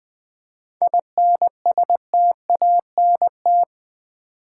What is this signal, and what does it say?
Morse code, heard as a keyed tone, about 20 wpm, 700 Hz. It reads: INSTANT
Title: 7 Letter Words - First 3 Letters Given - 20wpm